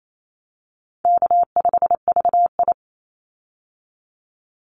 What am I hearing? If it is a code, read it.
Morse code, heard as a keyed tone, about 28 wpm, 700 Hz. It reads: K5VI
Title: Easy Call Signs Repeated 2x - 28wpm